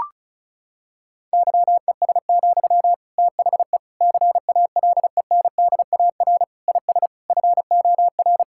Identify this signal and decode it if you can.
Morse code, heard as a keyed tone, about 36 wpm, 700 Hz. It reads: YES, THE CALENDAR IS FOR
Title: Sentences from Top 1600 Words 35wpm